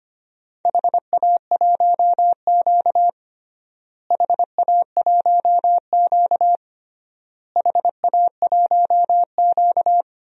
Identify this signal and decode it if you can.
Morse code, heard as a keyed tone, about 25 wpm, 700 Hz. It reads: HA1Q HA1Q HA1Q
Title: DXCC-100 Call Signs - 4 Characters - Repeated 3x - 25wpm